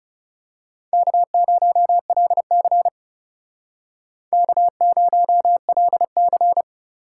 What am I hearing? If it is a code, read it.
Morse code, heard as a keyed tone, about 34 wpm, 700 Hz. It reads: K0LC K0LC
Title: Speed Racing - USA 1x2 Call Signs 30wpm